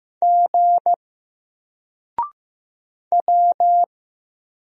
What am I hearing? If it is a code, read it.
Morse code, heard as a keyed tone, about 15 wpm, 700 Hz. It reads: G W
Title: Single Letter-Number Alt 15wpm